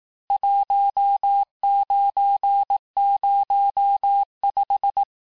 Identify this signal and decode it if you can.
Morse code, heard as a keyed tone, about 18 wpm, 800 Hz. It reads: 1905
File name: cours-du-lundi files / cours-1594--teste-clair-vitesse-17mpm-puis-22mpm